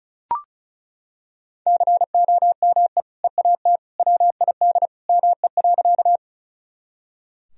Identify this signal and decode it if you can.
Morse code, heard as a keyed tone, about 34 wpm, 700 Hz. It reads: COME EAT WID ME.
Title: Sentences from Top 300 Words and Abbreviations 35wpm